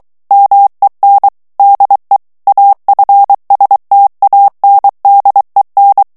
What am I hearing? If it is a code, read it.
Morse code, heard as a keyed tone, about 23 wpm, 800 Hz. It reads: MEN DE AFSTANDEN
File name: HOLLANDAIS texte clair 1 vitesse 22 mots minute